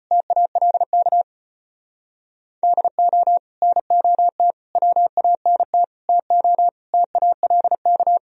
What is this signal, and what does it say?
Morse code, heard as a keyed tone, about 31 wpm, 700 Hz. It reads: TALK DO NOT WANT TO TALK